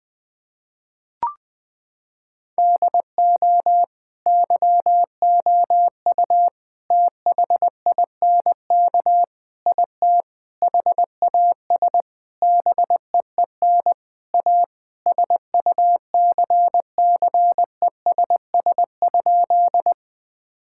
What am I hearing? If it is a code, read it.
Morse code, heard as a keyed tone, about 20 wpm, 700 Hz. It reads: DO YOU THINK IT HAS BEEN A SUCCESS?
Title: Sentences from Top 1000 Words - Review Part 6 - 20wpm